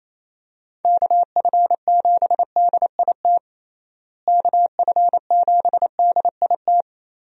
Morse code, heard as a keyed tone, about 27 wpm, 700 Hz. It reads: KF7DIT KF7DIT
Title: USA Call Signs - 6 Characters - Repeated 3x - 28wpm